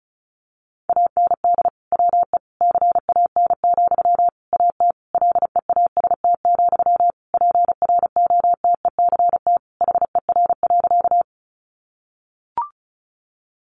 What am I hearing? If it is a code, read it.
Morse code, heard as a keyed tone, about 34 wpm, 700 Hz. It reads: AND WE CAN, AT LEAST, PROTECT HER.
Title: Sentences from Top 900 Words 35wpm